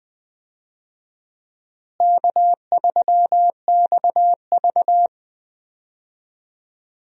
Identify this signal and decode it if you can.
Morse code, heard as a keyed tone, about 20 wpm, 700 Hz. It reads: K3XV
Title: USA Call Signs - 4 Characters - Repeated 2x - 20wpm